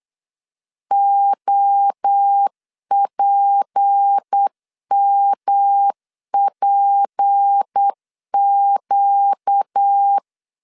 Morse code, heard as a keyed tone, about 8 wpm, 800 Hz. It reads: OPMPQ
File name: LEZIONI / 6 PKX0 / 6 P 8wpm 3025